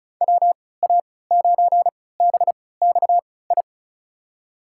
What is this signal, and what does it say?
Morse code, heard as a keyed tone, about 36 wpm, spaced as for 20 wpm, 700 Hz. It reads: WA9BXI
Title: USA Call Signs - 6 Characters - 3x Character Spacing - 35wpm